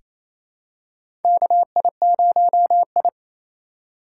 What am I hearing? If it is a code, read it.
Morse code, heard as a keyed tone, about 27 wpm, 700 Hz. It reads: KI0I
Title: K1USN SST QSO Practice - 28wpm